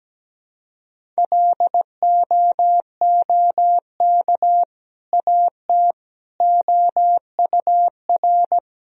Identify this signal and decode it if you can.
Morse code, heard as a keyed tone, about 17 wpm, 700 Hz. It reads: LOOK AT OUR
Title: Sentences from Top 1000 Words - Review Part 3 - 17wpm